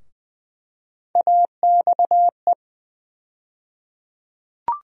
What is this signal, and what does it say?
Morse code, heard as a keyed tone, about 19 wpm, 700 Hz. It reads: AXE
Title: USA Call Sign Postfixes - 3 Letters 20wpm